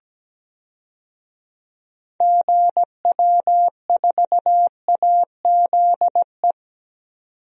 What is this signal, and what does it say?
Morse code, heard as a keyed tone, about 17 wpm, 700 Hz. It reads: GW4AZE
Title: DXCC-100 Call Signs - 6 Characters - 17wpm